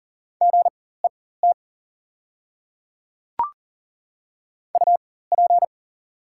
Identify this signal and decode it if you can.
Morse code, heard as a keyed tone, about 40 wpm, spaced as for 19 wpm, 700 Hz. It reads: GET UP
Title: 4x Character Spacing - Top 200 Words - 40wpm